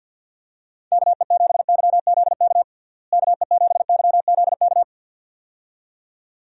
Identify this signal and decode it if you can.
Morse code, heard as a keyed tone, about 48 wpm, 700 Hz. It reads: KE7YZK KE7YZK
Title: USA Call Signs - 6 Characters - Repeated 2x - 50wpm